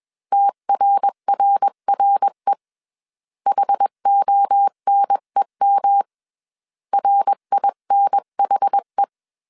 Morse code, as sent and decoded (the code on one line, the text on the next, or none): - .-. .-. .-. / .... --- -. -- / .-. .. -. .... .
TRRRE HONEM RINHE